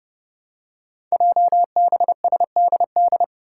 Morse code, heard as a keyed tone, about 30 wpm, 700 Hz. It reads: JBSDD
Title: Five Challenging Letter-Number Combinations 30wpm